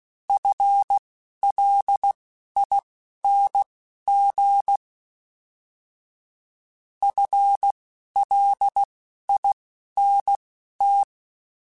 Morse code, heard as a keyed tone, about 16 wpm, spaced as for 11 wpm, 800 Hz. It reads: FLING FLINT
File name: FL Runs / FL Runs mp3s / FL Runs @10